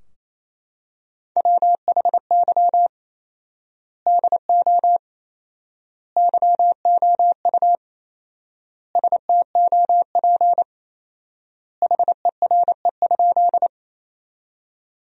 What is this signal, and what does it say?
Morse code, heard as a keyed tone, about 28 wpm, 700 Hz. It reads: WHY DO YOU STOP HERE?
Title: 4x Word-Spacing - Sentences from Top 300 Words - 28wpm